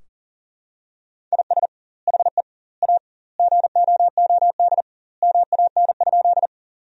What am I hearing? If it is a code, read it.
Morse code, heard as a keyed tone, about 40 wpm, 700 Hz. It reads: IS HE A GOOD MAN?
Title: 2x Word-Spacing - Sentences from Top 200 Words - 40wpm